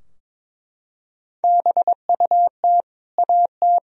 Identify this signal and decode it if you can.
Morse code, heard as a keyed tone, about 22 wpm, 700 Hz. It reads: BUT AT
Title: Sentences from Top 1000 Words - Review Part 3 - 22wpm